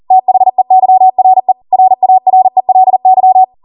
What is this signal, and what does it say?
Morse code, heard as a keyed tone, about 40 wpm, 750 Hz. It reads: THEYRE RARELY